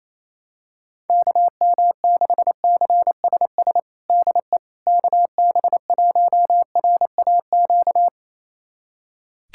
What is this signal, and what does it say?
Morse code, heard as a keyed tone, about 28 wpm, 700 Hz. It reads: KM6CSS DE KB1RAQ
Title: USA Callsign DE Callsign 28wpm